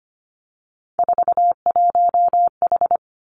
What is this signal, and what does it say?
Morse code, heard as a keyed tone, about 25 wpm, 700 Hz. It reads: 41H
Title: Three Challenging Letter-Number Combinations 25wpm